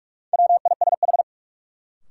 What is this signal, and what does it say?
Morse code, heard as a keyed tone, about 46 wpm, 700 Hz. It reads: WISH
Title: Sentences from Top 1600 Words - New Word Drill - 45wpm